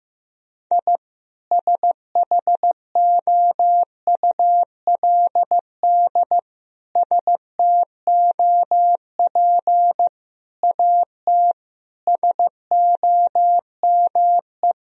Morse code, heard as a keyed tone, about 15 wpm, 700 Hz. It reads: I SHOULD STOP AT SOME
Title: Sentences from Top 1000 Words - Review Part 1 - 15wpm